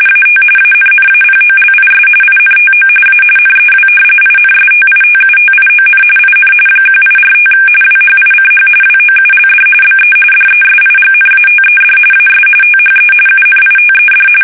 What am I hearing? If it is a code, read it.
RTTY, about 45.45 baud. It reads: AT ANNAPOLIS INTO A NEW INSTITUTION IN 1845 WHICH WOULD BE DESIGNATED AS THE UNITED
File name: fsk850 / usnavy2